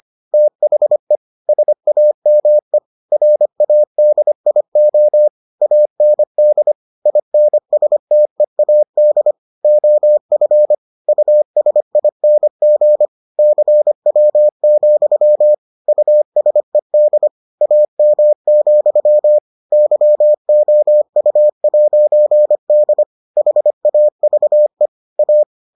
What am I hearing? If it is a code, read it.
Morse code, heard as a keyed tone, about 25 wpm, 600 Hz. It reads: THE SAME RADIO AND INSTEAD OF USING CW, USED AM, YOU'D HAVE A